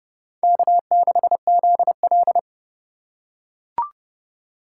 Morse code, heard as a keyed tone, about 28 wpm, 700 Hz. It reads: K6ZL